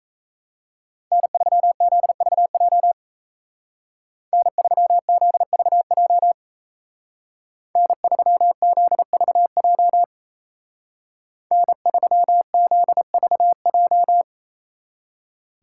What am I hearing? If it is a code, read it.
Morse code, heard as a keyed tone, about 33 wpm, 700 Hz. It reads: N3ZVJ N3ZVJ N3ZVJ N3ZVJ
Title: Speed Racing - USA 1x3 Call Signs 28wpm